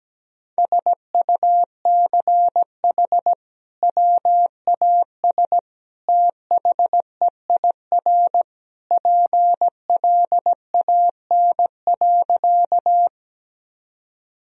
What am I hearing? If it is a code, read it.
Morse code, heard as a keyed tone, about 17 wpm, 700 Hz. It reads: SUCH WAS THEIR PLAN.
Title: Sentences from Top 1000 Words - Review Part 2 - 17wpm